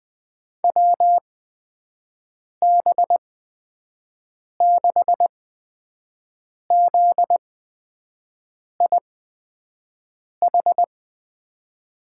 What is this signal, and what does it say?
Morse code, heard as a keyed tone, about 20 wpm, spaced as for 5 wpm, 700 Hz. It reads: WB6ZIH
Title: USA Call Signs - 6 Characters - 8x Character Spacing - 20wpm